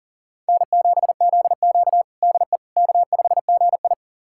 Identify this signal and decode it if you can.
Morse code, heard as a keyed tone, about 40 wpm, 700 Hz. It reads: N7ZQ DE K5GI
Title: USA Callsign DE Callsign - 4 Characters - 40wpm